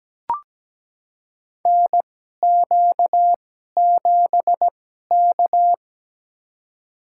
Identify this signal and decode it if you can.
Morse code, heard as a keyed tone, about 17 wpm, spaced as for 12 wpm, 700 Hz. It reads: NQ7K
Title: USA Call Signs - 4 Characters - 2x Character Spacing - 17wpm